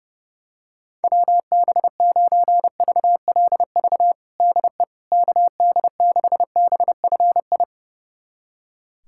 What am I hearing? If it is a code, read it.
Morse code, heard as a keyed tone, about 30 wpm, 700 Hz. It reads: WB9VLV DE KD6BFI
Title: USA Callsign DE Callsign 30wpm